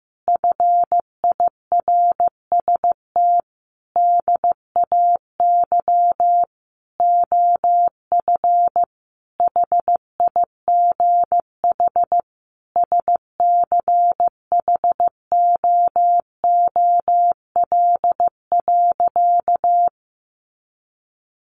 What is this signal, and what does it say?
Morse code, heard as a keyed tone, about 15 wpm, 700 Hz. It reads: FIRST DAY OF HIGH SCHOOL.
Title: Additional Easy Phrases 15wpm